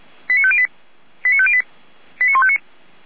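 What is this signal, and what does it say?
Five-tone selective call (ZVEI), heard as quick runs of five tones, tones of about 70 ms. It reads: 89508 89508 89159
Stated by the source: ZVEI five-tone selective call 89165 received with an SDR.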